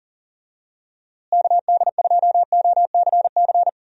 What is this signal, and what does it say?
Morse code, heard as a keyed tone, about 40 wpm, 700 Hz. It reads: KD2OCC
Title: USA Call Signs - 6 Characters - Repeated 2x - 40wpm